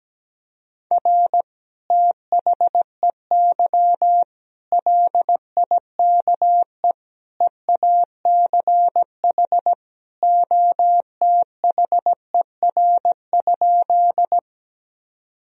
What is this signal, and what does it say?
Morse code, heard as a keyed tone, about 17 wpm, 700 Hz. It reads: R THEY LIKE EACH OTHER?
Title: Sentences from Top 200 Words and Abbreviations 17wpm